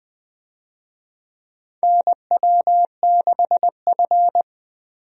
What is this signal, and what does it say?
Morse code, heard as a keyed tone, about 20 wpm, 700 Hz. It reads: NW6F
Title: USA Call Signs - 4 Characters - 20wpm